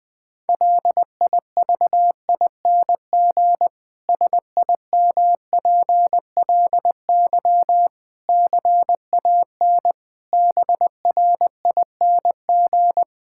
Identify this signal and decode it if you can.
Morse code, heard as a keyed tone, about 20 wpm, 700 Hz. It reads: LIVING SIMPLY CAN BRING
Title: Difficult Sentences with Hints - 20wpm